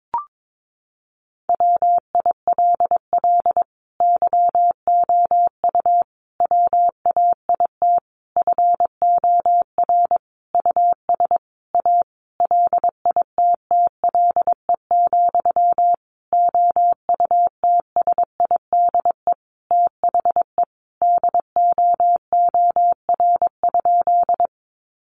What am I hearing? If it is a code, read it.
Morse code, heard as a keyed tone, about 22 wpm, 700 Hz. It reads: WILL YOU WAIT FOR US A LITTLE, OUTSIDE THE DOOR?
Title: Sentences from Top 900 Words 22wpm